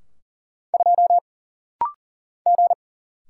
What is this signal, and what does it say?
Morse code, heard as a keyed tone, about 39 wpm, 700 Hz. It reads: 2 G
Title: Single Letter-Number Alt - Rapid-Fire 40wpm